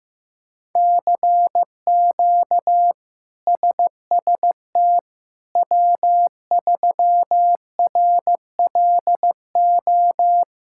Morse code, heard as a keyed tone, about 15 wpm, 700 Hz. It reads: CQ SST W3RLO
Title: K1USN SST QSO Practice - 15wpm